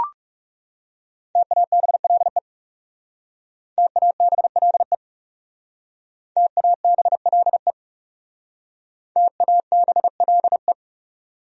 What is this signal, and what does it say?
Morse code, heard as a keyed tone, about 36 wpm, 700 Hz. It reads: TABLE TABLE TABLE TABLE
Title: Speed-Racing Top 500 Words 30wpm